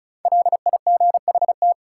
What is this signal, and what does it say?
Morse code, heard as a keyed tone, about 36 wpm, 700 Hz. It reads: LIGHT
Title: Sentences from Top 200 Words - New Word Drill - 35wpm